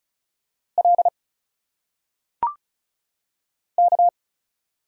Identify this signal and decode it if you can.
Morse code, heard as a keyed tone, about 35 wpm, 700 Hz. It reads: L K
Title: Single Letters - Phonetic Pronunciation 35wpm